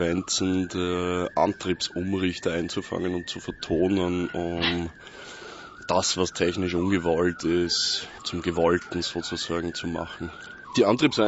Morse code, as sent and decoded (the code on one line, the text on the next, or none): none